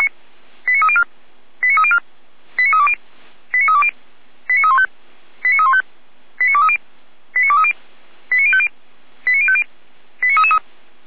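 Five-tone selective call (ZVEI), heard as quick runs of five tones, tones of about 80 ms. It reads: 89304 89304 89320 89320 89315 89315 89230 89230 89050 89050 89333
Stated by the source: Five-tone ZVEI selective call received on an SDR.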